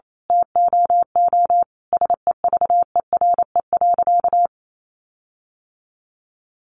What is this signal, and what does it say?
Morse code, heard as a keyed tone, about 28 wpm, 700 Hz. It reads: TOO SEVERE.
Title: Sentences from Top 1400 Words 28wpm